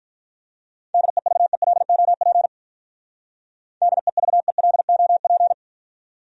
Morse code, heard as a keyed tone, about 42 wpm, 700 Hz. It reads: DEVELOP DEVELOP
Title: Speed-Racing Top 500 Words 35wpm